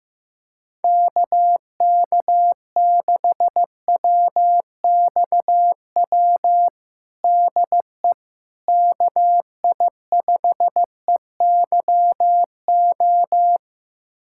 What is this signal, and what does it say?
Morse code, heard as a keyed tone, about 15 wpm, 700 Hz. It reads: KK6WXW DE KI5EYO
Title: USA Callsign DE Callsign - 6 Characters - 15wpm